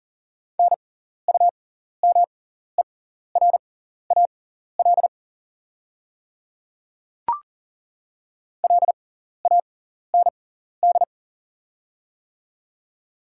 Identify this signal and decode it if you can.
Morse code, heard as a keyed tone, about 39 wpm, spaced as for 14 wpm, 700 Hz. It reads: NUMERAL LAND
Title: 6x Character Spacing - Top 500 Words - 40wpm